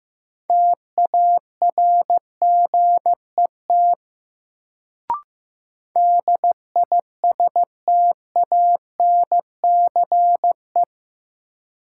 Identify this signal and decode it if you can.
Morse code, heard as a keyed tone, about 15 wpm, 700 Hz. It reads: TARGET DISTANCE
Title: Sentences from Top 1200 Words - New Word Drill Alt - 15wpm